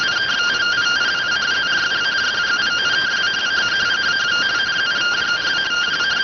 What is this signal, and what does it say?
Two-tone teleprinter signal, about 170 Hz shift.